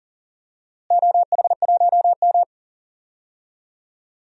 Morse code, heard as a keyed tone, about 40 wpm, 700 Hz. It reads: OH1M
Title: DXCC-100 Call Signs - 4 Characters - Repeated 3x - 40wpm